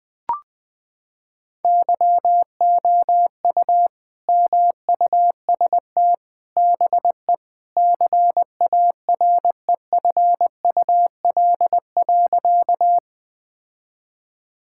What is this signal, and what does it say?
Morse code, heard as a keyed tone, about 20 wpm, 700 Hz. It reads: YOU MUST BE CAREFUL.
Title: Sentences from Top 1600 Words 20wpm